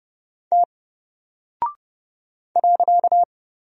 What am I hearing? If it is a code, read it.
Morse code, heard as a keyed tone, about 30 wpm, 700 Hz. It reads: T .
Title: Single Character (Letter:Number:Punctuation:Prosign) Alt 30wpm